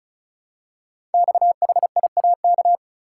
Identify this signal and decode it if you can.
Morse code, heard as a keyed tone, about 36 wpm, 700 Hz. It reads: XHIAK
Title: Code Groups - 5 Letter:Numbers - 35wpm